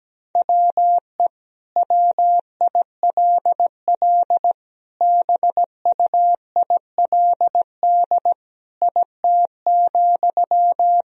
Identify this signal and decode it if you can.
Morse code, heard as a keyed tone, about 17 wpm, 700 Hz. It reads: WE WILL BUILD IT,
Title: Sentences from Top 1000 Words - Review Part 1 - 17wpm